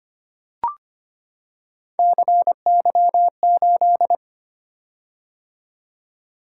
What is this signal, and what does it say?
Morse code, heard as a keyed tone, about 25 wpm, 700 Hz. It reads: CY8
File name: Three Letter-Number Combinations - 25wpm